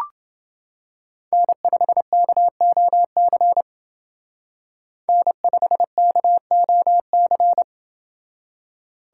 Morse code, heard as a keyed tone, about 28 wpm, 700 Hz. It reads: N5KOC N5KOC